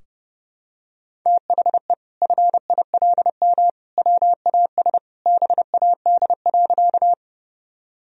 Morse code, heard as a keyed tone, about 30 wpm, 700 Hz. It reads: THE FILM WAS BAD.
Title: Sentences from Top 1000 Words - Review Part 3 - 30wpm